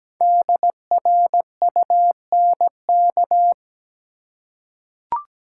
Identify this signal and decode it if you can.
Morse code, heard as a keyed tone, about 17 wpm, 700 Hz. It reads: DRUNK
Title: Five Letter Words - 17wpm